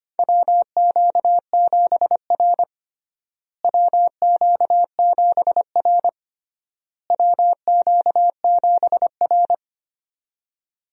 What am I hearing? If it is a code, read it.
Morse code, heard as a keyed tone, about 25 wpm, 700 Hz. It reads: WQ7R WQ7R WQ7R
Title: USA Call Signs - 4 Characters - Repeated 3x - 25wpm